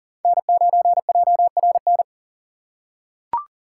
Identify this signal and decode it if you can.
Morse code, heard as a keyed tone, about 40 wpm, 700 Hz. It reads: N9JRN
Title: USA Call Signs - 4x Character Spacing - 40wpm